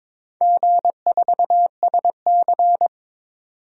Morse code, heard as a keyed tone, about 22 wpm, 700 Hz. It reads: G4SC
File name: practice / 22wpm / Four Letter-Number Combinations - 22wpm